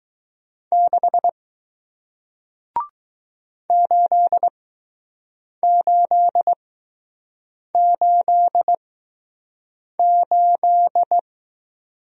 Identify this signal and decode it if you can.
Morse code, heard as a keyed tone, about 21 wpm, spaced as for 6 wpm, 700 Hz. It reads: 6 8888